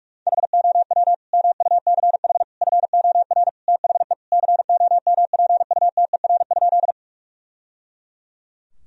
Morse code, heard as a keyed tone, about 46 wpm, 700 Hz. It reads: HOW MUCH FOR THE COMPUTER?
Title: Sentences from Top 1000 Words - Review Part 3 - 45wpm